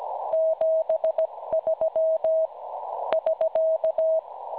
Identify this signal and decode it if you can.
Morse code, heard as a keyed tone, about 17 wpm, 640 Hz. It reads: 73 <SK>
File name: cwronde / websdr recording .6kHz